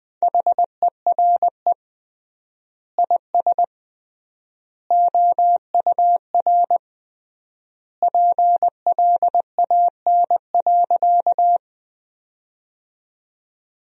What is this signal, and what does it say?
Morse code, heard as a keyed tone, about 20 wpm, 700 Hz. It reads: HERE IS OUR PLAN.
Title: 3x Word-Spacing - Sentences from Top 500 Words - 20wpm